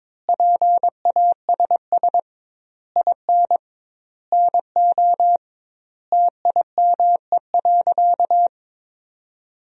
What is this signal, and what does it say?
Morse code, heard as a keyed tone, about 22 wpm, 700 Hz. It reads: PASS IN NO TIME.
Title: 2x Word-Spacing - Sentences from Top 400 Words - 22wpm